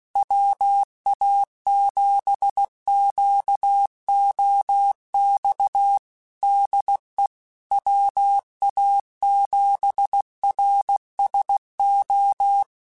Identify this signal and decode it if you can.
Morse code, heard as a keyed tone, about 16 wpm, 800 Hz. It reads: WA7QOX DE WA7RSO
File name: QSOs / QSOs-mp3s / WA7QOX 2@15